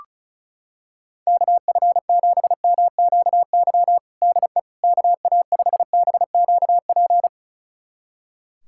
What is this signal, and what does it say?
Morse code, heard as a keyed tone, about 36 wpm, 700 Hz. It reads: KF7MQY DE KA5BQP
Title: USA Callsign DE Callsign - 6 Characters - 35wpm